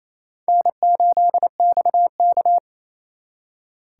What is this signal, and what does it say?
Morse code, heard as a keyed tone, about 36 wpm, 700 Hz. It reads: N8XK